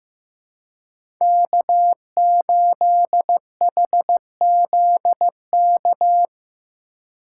Morse code, heard as a keyed tone, about 15 wpm, 700 Hz. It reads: K8HZK
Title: USA Call Signs - 5 Characters - 2x Character Spacing - 15wpm